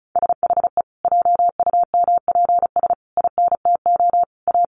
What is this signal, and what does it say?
Morse code, heard as a keyed tone, about 34 wpm, 700 Hz. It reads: SHE JUMPS INTO A
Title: Additional Easy Phrases 35wpm